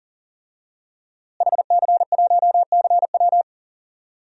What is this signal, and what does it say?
Morse code, heard as a keyed tone, about 37 wpm, 700 Hz. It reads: HC1CW
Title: DXCC-100 Call Signs - 5 Characters - Repeated 3x - 40wpm